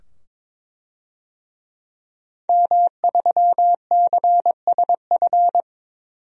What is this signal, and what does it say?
Morse code, heard as a keyed tone, about 22 wpm, 700 Hz. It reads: M3CSF
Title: DXCC-100 Call Signs - 5 Characters - 22wpm